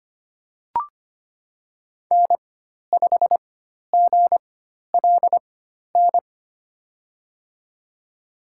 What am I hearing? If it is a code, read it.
Morse code, heard as a keyed tone, about 25 wpm, spaced as for 12 wpm, 700 Hz. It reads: N5GLN
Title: USA Call Signs - 5 Characters - 4x Character Spacing - 25wpm